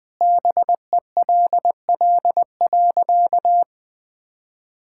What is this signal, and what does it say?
Morse code, heard as a keyed tone, about 19 wpm, 700 Hz. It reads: BELL.
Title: Sentences with 2-Word Phrases - 20wpm